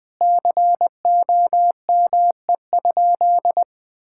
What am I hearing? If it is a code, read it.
Morse code, heard as a keyed tone, about 20 wpm, 700 Hz. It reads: COME?
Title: Sentences with 5-Word Phrases - Spoken Beginning - 20wpm